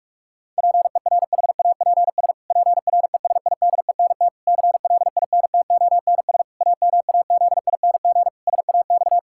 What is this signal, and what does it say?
Morse code, heard as a keyed tone, about 41 wpm, 700 Hz. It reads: PERHAPS PRESIDENT CLINTONS AMAZING SAX
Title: Pangram Challenge v2 - 45wpm